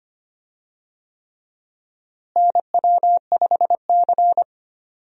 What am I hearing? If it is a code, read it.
Morse code, heard as a keyed tone, about 25 wpm, 700 Hz. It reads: NW5C
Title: USA Call Signs - 4 Characters - Repeated 3x - 25wpm